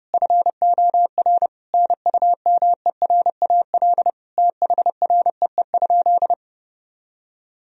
Morse code, heard as a keyed tone, about 32 wpm, 700 Hz. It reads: FOR NUMERAL THREE?
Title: Sentences from Top 1000 Words - Review Part 2 - 30wpm